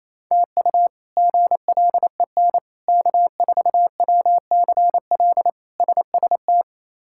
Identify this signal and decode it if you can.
Morse code, heard as a keyed tone, about 28 wpm, 700 Hz. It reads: TU GLEN K4WCL SST
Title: K1USN SST QSO Practice - 28wpm